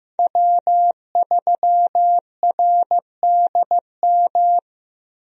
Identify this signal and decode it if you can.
Morse code, heard as a keyed tone, about 15 wpm, 700 Hz. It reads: W3RDM
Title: USA Call Signs - 5 Characters - Repeated 2x - 15wpm